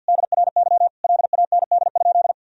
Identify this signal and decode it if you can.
Morse code, heard as a keyed tone, about 50 wpm, 700 Hz. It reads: DRY LAND?
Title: Sentences from Top 1000 Words - Review Part 3 - 50wpm